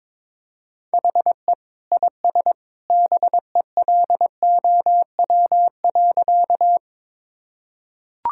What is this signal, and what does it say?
Morse code, heard as a keyed tone, about 21 wpm, 700 Hz. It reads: HE IS BELOW.
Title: Sentences from Top 1000 Words - Review Part 5 - 22wpm